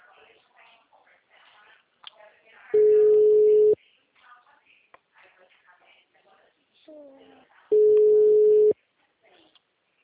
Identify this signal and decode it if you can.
Ringback tone (Europe, 425 Hz), with speech.
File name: out- -2011- - - .18768